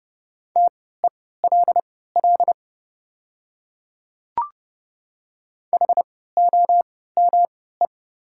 Morse code, heard as a keyed tone, about 30 wpm, spaced as for 17 wpm, 690 Hz. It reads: TELL HOME
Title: 3x Character Spacing - Top 300 Words - 30wpm